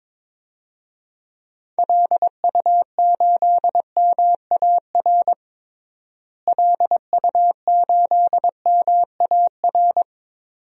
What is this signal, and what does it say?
Morse code, heard as a keyed tone, about 22 wpm, 700 Hz. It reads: LU8MAR LU8MAR
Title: DXCC-100 Call Signs - 6 Characters - Repeated 2x - 22wpm